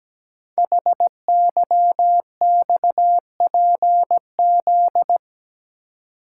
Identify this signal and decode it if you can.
Morse code, heard as a keyed tone, about 17 wpm, 700 Hz. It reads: HYXPZ
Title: Code Groups - 5 Letters - 17wpm